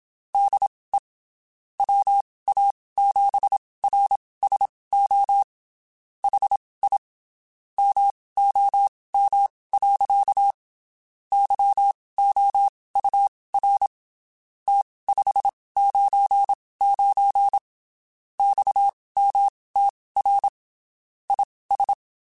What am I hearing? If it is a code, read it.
Morse code, heard as a keyed tone, about 27 wpm, spaced as for 19 wpm, 800 Hz. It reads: DE WA7RSO HI MOM. YOUR T599 XMTR IS
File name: (spaced) / QSOs Spaced-mp3s / WA7QOX 1 @25 sp